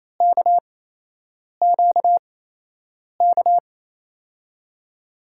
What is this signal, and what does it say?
Morse code, heard as a keyed tone, about 28 wpm, spaced as for 8 wpm, 700 Hz. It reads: KQK
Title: USA Call Signs - 6 Characters - 8x Character Spacing - 28wpm